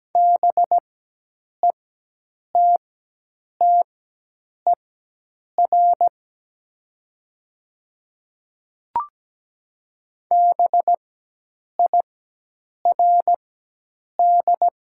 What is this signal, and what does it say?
Morse code, heard as a keyed tone, about 17 wpm, spaced as for 8 wpm, 700 Hz. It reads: BETTER BIRD
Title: 4x Character Spacing - Top 500 Words - 17wpm